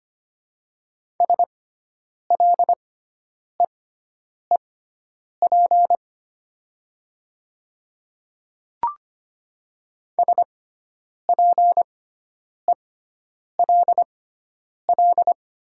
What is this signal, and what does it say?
Morse code, heard as a keyed tone, about 25 wpm, spaced as for 9 wpm, 700 Hz. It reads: SLEEP SPELL
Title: 6x Character Spacing - Top 500 Words - 25wpm